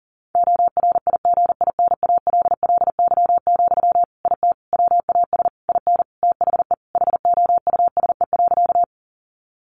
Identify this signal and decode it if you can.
Morse code, heard as a keyed tone, about 40 wpm, 700 Hz. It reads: ORIGINALLY, IT WAS IN THE HOUSE.
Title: Sentences from Top 1500 Words 40wpm